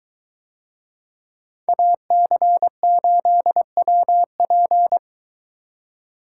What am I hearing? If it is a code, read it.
Morse code, heard as a keyed tone, about 23 wpm, 700 Hz. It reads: AC8WP